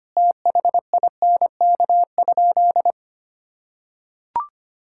Morse code, heard as a keyed tone, about 25 wpm, 700 Hz. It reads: THINK?
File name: practice / 25wpm / Sentences with 3-Word Phrases - 25wpm